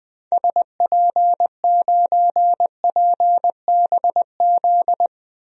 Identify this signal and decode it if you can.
Morse code, heard as a keyed tone, about 20 wpm, 700 Hz. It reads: SP9PBZ
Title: DXCC-100 Call Signs - 6 Characters - Repeated 2x - 20wpm